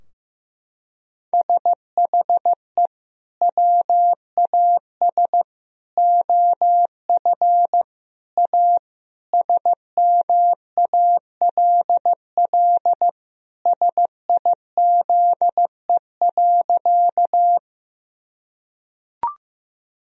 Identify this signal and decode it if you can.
Morse code, heard as a keyed tone, about 16 wpm, 700 Hz. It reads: SHE WAS OF A SMALL SIZE.
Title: Sentences from Top 1000 Words - Review Part 2 - 15wpm